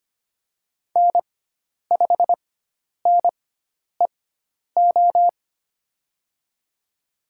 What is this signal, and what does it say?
Morse code, heard as a keyed tone, about 25 wpm, spaced as for 10 wpm, 700 Hz. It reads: N5NEO
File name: Lesson 16g - Letter c - Call Signs 25wpm